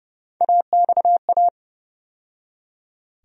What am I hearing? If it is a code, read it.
Morse code, heard as a keyed tone, about 30 wpm, 700 Hz. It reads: AXA
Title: USA Call Sign Postfixes - 3 Letters 30wpm